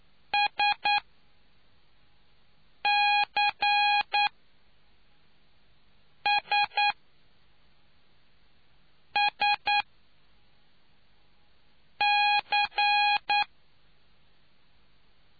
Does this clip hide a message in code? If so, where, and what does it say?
Yes: Morse code from 0.3 to 13.4 s, heard as a keyed tone, about 9 wpm, spaced as for 4 wpm, 820 Hz. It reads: SCSSC